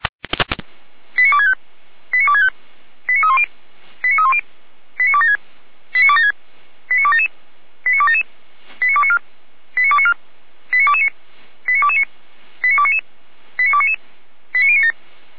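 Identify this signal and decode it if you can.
Five-tone selective call (ZVEI), heard as quick runs of five tones, tones of about 70 ms. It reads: 89265 89265 89310 89310 89276 89276 89277 89277 89294 89294 89229 89229 89299 89299 89097
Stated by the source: ZVEI five-tone selective call 89160 received on SDR, from the Feuerwehr Orlamünde fire brigade audio server.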